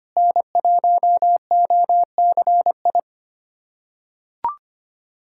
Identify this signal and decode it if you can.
Morse code, heard as a keyed tone, about 25 wpm, 700 Hz. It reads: N1OCI
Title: USA Callsign DE Callsign 25wpm